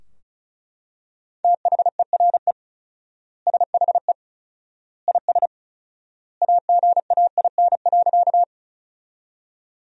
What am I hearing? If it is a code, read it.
Morse code, heard as a keyed tone, about 34 wpm, 700 Hz. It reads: THERE SHE IS AGAIN.
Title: 4x Word-Spacing - Sentences from Top 200 Words - 35wpm